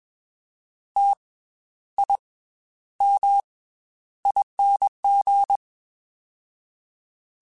Morse code, heard as a keyed tone, about 21 wpm, 800 Hz. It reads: T I M ING
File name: Special INGs / Special INGs-mp3s / Slow ing @20